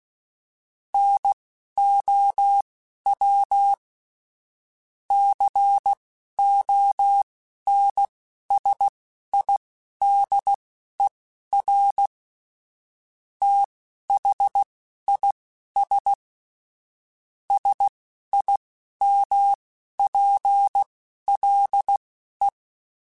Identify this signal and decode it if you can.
Morse code, heard as a keyed tone, about 16 wpm, spaced as for 11 wpm, 800 Hz. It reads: NOW CONSIDER THIS SIMPLE
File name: Technical Subjects / Technical Subjects-mp3s / OR Gate @10